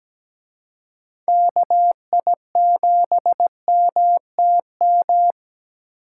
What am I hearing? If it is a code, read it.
Morse code, heard as a keyed tone, about 20 wpm, 700 Hz. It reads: KI7MTM
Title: Speed Racing - USA 2x3 Call Signs 17wpm